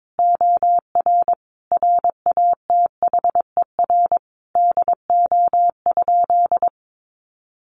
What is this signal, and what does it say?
Morse code, heard as a keyed tone, about 22 wpm, 700 Hz. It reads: OR RATHER DO?
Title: Sentences from Top 700 Words 22wpm